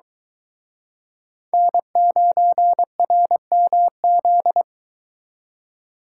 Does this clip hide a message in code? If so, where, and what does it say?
Yes: Morse code from 1.5 to 4.6 s, heard as a keyed tone, about 23 wpm, 700 Hz. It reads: N9RMZ